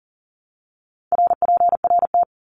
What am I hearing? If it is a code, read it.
Morse code, heard as a keyed tone, about 40 wpm, 700 Hz. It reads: RPRT
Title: Abbreviations Repeated 3x - 40wpm